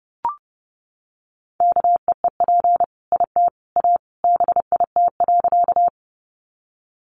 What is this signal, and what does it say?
Morse code, heard as a keyed tone, about 30 wpm, 700 Hz. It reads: KEEP IT A BIT.
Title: Sentences from Top 900 Words 30wpm